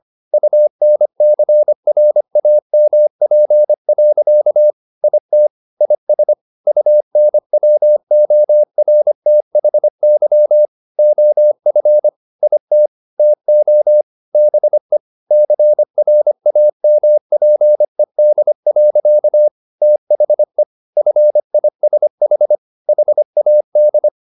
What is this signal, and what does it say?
Morse code, heard as a keyed tone, about 25 wpm, 600 Hz. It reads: UNCRAMP. IT IS UNWORTHY OF IT TO BE CRAMPED. THE FISH HAD
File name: the old man and the sea part 1 25wpm